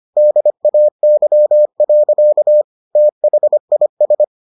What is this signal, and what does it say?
Morse code, heard as a keyed tone, about 25 wpm, 600 Hz. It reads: DAY. THIS